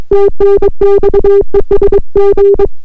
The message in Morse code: TNX ES G